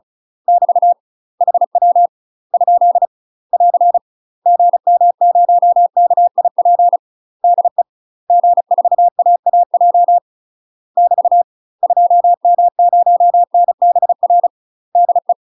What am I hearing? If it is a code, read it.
Morse code, heard as a keyed tone, about 34 wpm, 700 Hz. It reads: <BT> HW ? <AR> GM0KIP DE G4AAJ <BT> 2M0NBR DE
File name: UK MASTER QSOs TRAINING/35WPM